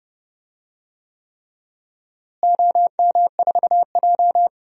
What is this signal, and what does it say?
Morse code, heard as a keyed tone, about 30 wpm, 700 Hz. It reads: OM4J
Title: DXCC-100 Call Signs - 4 Characters - Repeated 3x - 30wpm